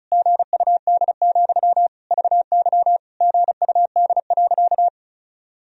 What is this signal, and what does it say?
Morse code, heard as a keyed tone, about 34 wpm, 700 Hz. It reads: GUD, VY GUD.
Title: Sentences from Top 200 Words and Abbreviations 35wpm